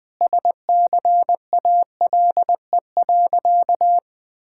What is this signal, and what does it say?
Morse code, heard as a keyed tone, about 20 wpm, 700 Hz. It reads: SCALE.
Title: Sentences from Top 1000 Words - Review Part 5 - 20wpm